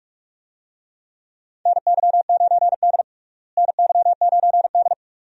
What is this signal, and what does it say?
Morse code, heard as a keyed tone, about 47 wpm, 700 Hz. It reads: NY9D NY9D
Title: USA Call Signs - 4 Characters - Repeated 2x - 45wpm